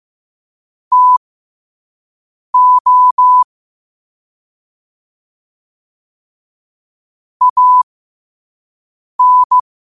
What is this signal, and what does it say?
Morse code, heard as a keyed tone, about 15 wpm, spaced as for 5 wpm, 1000 Hz. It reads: TO AN
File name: qotdfiles / 5wpm